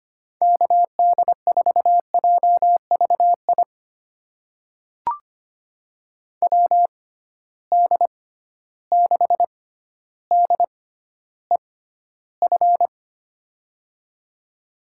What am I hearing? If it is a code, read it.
Morse code, heard as a keyed tone, about 25 wpm, spaced as for 13 wpm, 700 Hz. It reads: KD4JVI WD6DEF
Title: USA Call Signs - 6 Characters - 6x Character Spacing - 25wpm